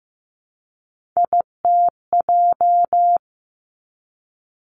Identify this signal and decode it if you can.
Morse code, heard as a keyed tone, about 15 wpm, 700 Hz. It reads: ITJ
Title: Three Challenging Letter-Number Combinations 15wpm